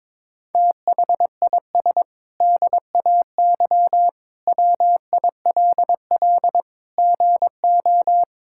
Morse code, heard as a keyed tone, about 22 wpm, 700 Hz. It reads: THIS DAY WILL GO
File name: practice / 22wpm / Sentences with 8-Word Phrases - 22wpm